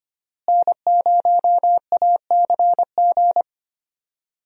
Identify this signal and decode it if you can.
Morse code, heard as a keyed tone, about 25 wpm, 700 Hz. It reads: N0ACG
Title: USA Call Signs - 5 Characters - Repeated 3x - 25wpm